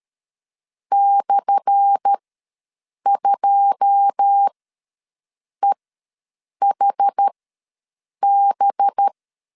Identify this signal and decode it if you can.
Morse code, heard as a keyed tone, about 13 wpm, spaced as for 7 wpm, 800 Hz. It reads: /2EHB
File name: LEZIONI / 10 SIGNS / 10A SLASH ws 12wpm 3443